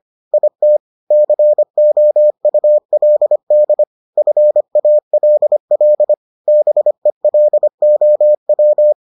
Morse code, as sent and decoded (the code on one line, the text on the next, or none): .. - / -.-. --- ..- .-.. -.. / ..-. .- .-.. .-.. / -... .-.. --- .--
IT COULD FALL BELOW